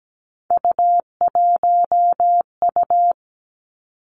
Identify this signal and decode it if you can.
Morse code, heard as a keyed tone, about 17 wpm, 700 Hz. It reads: U1U
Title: Three Challenging Letter-Number Combinations 17wpm